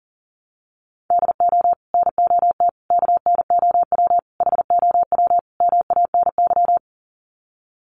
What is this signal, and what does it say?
Morse code, heard as a keyed tone, about 40 wpm, 700 Hz. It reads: DO NOT KNOW HOW MANY
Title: Sets of 5 Words Encore 40wpm